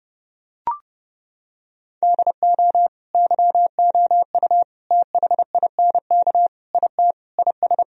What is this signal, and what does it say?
Morse code, heard as a keyed tone, about 30 wpm, 700 Hz. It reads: DO YOU THINK IT IS
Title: Sentences from Top 1000 Words - Review Part 4 - 30wpm